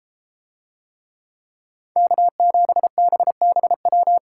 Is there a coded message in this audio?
Yes: Morse code from 2.0 to 4.2 s, heard as a keyed tone, about 31 wpm, 700 Hz. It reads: K7BBW